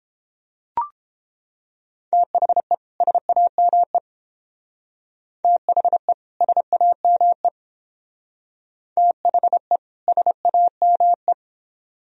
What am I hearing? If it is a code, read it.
Morse code, heard as a keyed tone, about 28 wpm, 700 Hz. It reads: THE SAME THE SAME THE SAME